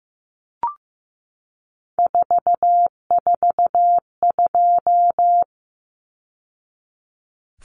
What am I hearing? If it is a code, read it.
Morse code, heard as a keyed tone, about 15 wpm, 700 Hz. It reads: 442
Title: Numbers - 3-Digit 15wpm